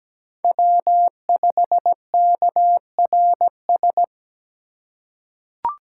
Morse code, heard as a keyed tone, about 17 wpm, 700 Hz. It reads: W5KRS
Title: USA Call Signs - 2x Character Spacing - 17wpm